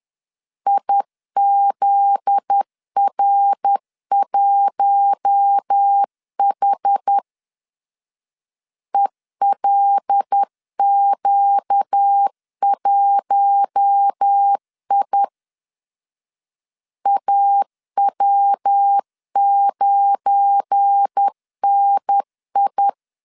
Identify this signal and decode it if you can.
Morse code, heard as a keyed tone, about 10 wpm, 800 Hz. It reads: IZR1H ELQ1I AW9NI
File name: LEZIONI / 7 GF9Z / 7 GF9Z 10wpm 2611